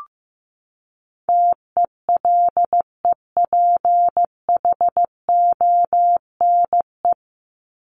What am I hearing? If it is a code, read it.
Morse code, heard as a keyed tone, about 15 wpm, 700 Hz. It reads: TELEPHONE
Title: Top 1501-1750 Words 15wpm